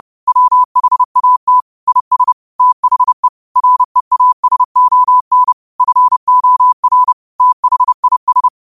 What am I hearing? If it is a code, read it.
Morse code, heard as a keyed tone, about 31 wpm, 1000 Hz. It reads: WHAT IS THE REASON FOR THIS
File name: qotdfiles / 30wpm